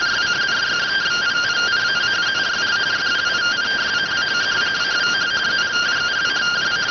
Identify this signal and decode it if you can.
Two-tone teleprinter signal, about 170 Hz shift.